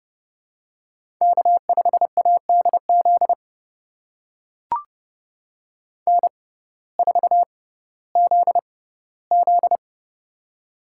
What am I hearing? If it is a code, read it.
Morse code, heard as a keyed tone, about 30 wpm, spaced as for 17 wpm, 700 Hz. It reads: K5ADZ N4ZZ
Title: USA Call Signs - 6x Character Spacing - 30wpm